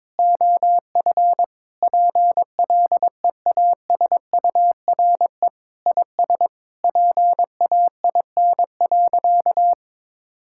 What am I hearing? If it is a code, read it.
Morse code, heard as a keyed tone, about 22 wpm, 700 Hz. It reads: OF PLEASURE IS PAIN.
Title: Sentences from Top 1600 Words 22wpm